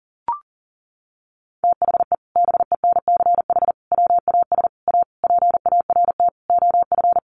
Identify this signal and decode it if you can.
Morse code, heard as a keyed tone, about 40 wpm, 700 Hz. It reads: THE BENCH WAS A PART OF
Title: Sentences from Top 1400 Words 40wpm